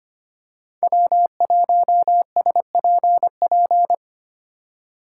Morse code, heard as a keyed tone, about 25 wpm, 700 Hz. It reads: W1SPP
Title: USA Call Signs - 2x Character Spacing - 25wpm